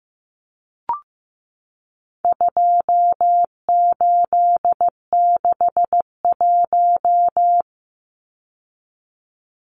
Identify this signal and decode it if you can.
Morse code, heard as a keyed tone, about 15 wpm, 700 Hz. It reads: 2861
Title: Numbers - 4-Digit 15wpm